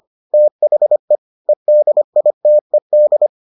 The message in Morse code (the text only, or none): THE EDITED